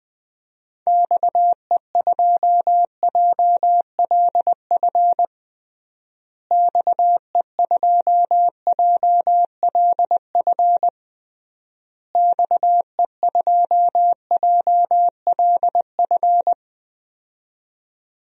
Morse code, heard as a keyed tone, about 21 wpm, 700 Hz. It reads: XE2JLF XE2JLF XE2JLF
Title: DXCC-100 Call Signs - 6 Characters - Repeated 3x - 20wpm